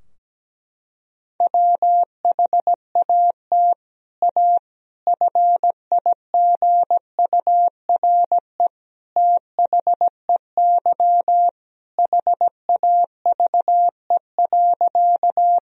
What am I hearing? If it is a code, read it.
Morse code, heard as a keyed tone, about 17 wpm, 700 Hz. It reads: WHAT A FIGURE THEY HAVE.
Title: Sentences from Top 1000 Words - Review Part 2 - 17wpm